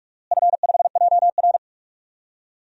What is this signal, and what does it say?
Morse code, heard as a keyed tone, about 46 wpm, 700 Hz. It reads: F5JR
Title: DXCC-100 Call Signs - 4 Characters - Repeated 3x - 45wpm